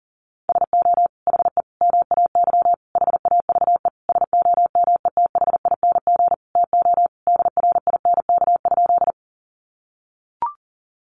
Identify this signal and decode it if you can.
Morse code, heard as a keyed tone, about 40 wpm, 700 Hz. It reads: SO HE MAY HAVE SOMETHING TO DRINK?
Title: Sentences from Top 1500 Words 40wpm